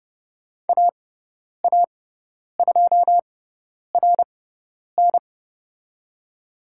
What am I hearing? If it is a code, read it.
Morse code, heard as a keyed tone, about 30 wpm, spaced as for 10 wpm, 690 Hz. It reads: AA2RN
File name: Lesson 18g - Letter m - Call Signs 30wpm